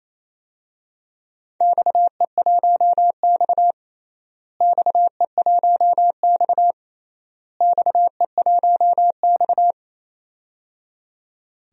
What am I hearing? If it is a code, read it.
Morse code, heard as a keyed tone, about 28 wpm, 700 Hz. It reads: XE1X XE1X XE1X
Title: DXCC-100 Call Signs - 4 Characters - Repeated 3x - 28wpm